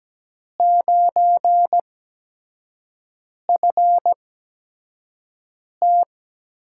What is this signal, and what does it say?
Morse code, heard as a keyed tone, about 17 wpm, spaced as for 5 wpm, 700 Hz. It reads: 9FT
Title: USA Call Signs - 4 Characters - 8x Character Spacing - 17wpm